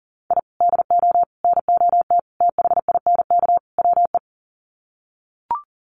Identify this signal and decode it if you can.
Morse code, heard as a keyed tone, about 37 wpm, 700 Hz. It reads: I DO NOT THINK WE
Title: Sets of 5 Words 40wpm